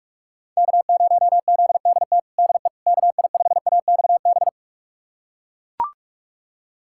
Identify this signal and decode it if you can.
Morse code, heard as a keyed tone, about 41 wpm, 700 Hz. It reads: K0ZDT DE KI5AXB
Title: USA Callsign DE Callsign 45wpm